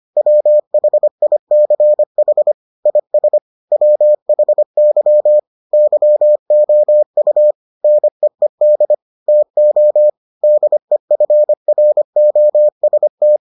Morse code, heard as a keyed tone, about 25 wpm, 600 Hz. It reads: WHICH IS WHY YOU NEED TO DEFROST